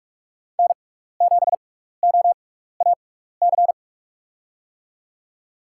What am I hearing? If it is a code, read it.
Morse code, heard as a keyed tone, about 46 wpm, spaced as for 16 wpm, 700 Hz. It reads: N7OAC
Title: USA Call Signs - 6x Character Spacing - 45wpm